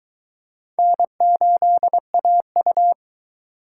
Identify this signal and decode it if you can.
Morse code, heard as a keyed tone, about 22 wpm, 700 Hz. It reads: N8AU